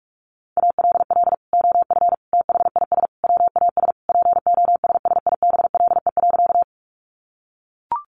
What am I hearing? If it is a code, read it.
Morse code, heard as a keyed tone, about 44 wpm, 700 Hz. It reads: ALL OF THIS WAS POSSIBLE.
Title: Sentences from Top 500 Words 45wpm